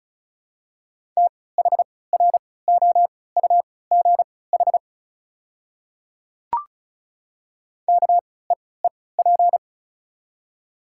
Morse code, heard as a keyed tone, about 36 wpm, spaced as for 20 wpm, 700 Hz. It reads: THROUGH KEEP
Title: 3x Character Spacing - Top 300 Words - 35wpm